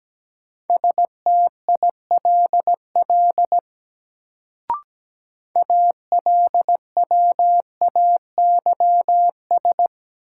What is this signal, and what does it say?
Morse code, heard as a keyed tone, about 17 wpm, 700 Hz. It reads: STILL ALWAYS
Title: Sentences from Top 300 Words - New Word Drill - 17wpm